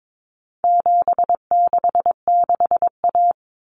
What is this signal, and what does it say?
Morse code, heard as a keyed tone, about 22 wpm, 700 Hz. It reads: 766A
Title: Common Cut Numbers - 4-Digit 22wpm